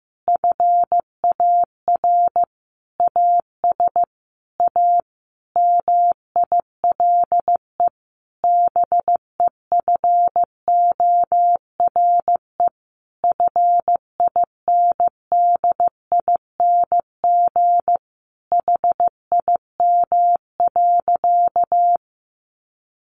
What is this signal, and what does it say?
Morse code, heard as a keyed tone, about 15 wpm, 700 Hz. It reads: FAR AS A MILE BEFORE FINDING HIM.
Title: Sentences from Top 1100 Words 15wpm